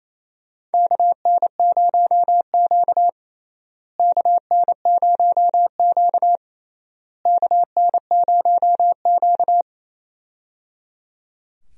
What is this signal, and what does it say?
Morse code, heard as a keyed tone, about 28 wpm, 700 Hz. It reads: KN0Q KN0Q KN0Q
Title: USA Call Signs - 4 Characters - Repeated 3x - 28wpm